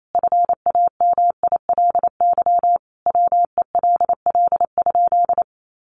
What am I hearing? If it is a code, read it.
Morse code, heard as a keyed tone, about 28 wpm, 700 Hz. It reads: FAMILY WELL?
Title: Sentences from Top 400 Words 28wpm